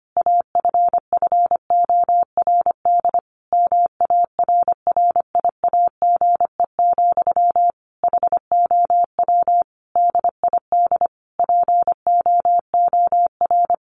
Morse code, heard as a keyed tone, about 25 wpm, 700 Hz. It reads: AFFORD MARRIAGE, HOW DID POOR
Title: Sentences from Top 1100 Words 25wpm